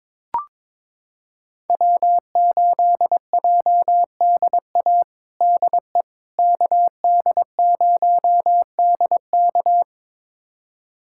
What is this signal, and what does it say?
Morse code, heard as a keyed tone, about 22 wpm, 700 Hz. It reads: W8JDA DE KD0DK
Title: USA Callsign DE Callsign - 5 Characters - 22wpm